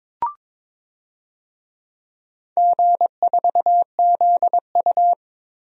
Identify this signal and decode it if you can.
Morse code, heard as a keyed tone, about 22 wpm, 700 Hz. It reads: G4ZU
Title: DXCC-100 Call Signs - 4 Characters - 22wpm